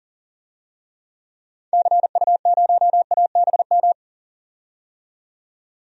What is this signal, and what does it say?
Morse code, heard as a keyed tone, about 39 wpm, 700 Hz. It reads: CU0ABM
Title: DXCC-100 Call Signs - 6 Characters - 40wpm